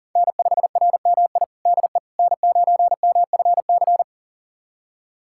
Morse code, heard as a keyed tone, about 37 wpm, 700 Hz. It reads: N5RMI DE N9MFC
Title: USA Callsign DE Callsign - 5 Characters - 40wpm